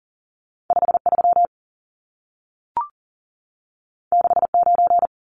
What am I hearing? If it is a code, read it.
Morse code, heard as a keyed tone, about 40 wpm, 700 Hz. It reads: 53 69
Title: Numbers - 2-Digit 40wpm